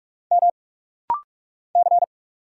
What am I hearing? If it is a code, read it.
Morse code, heard as a keyed tone, about 44 wpm, 700 Hz. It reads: M C
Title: Single Letter-Number Alt - Rapid-Fire 45wpm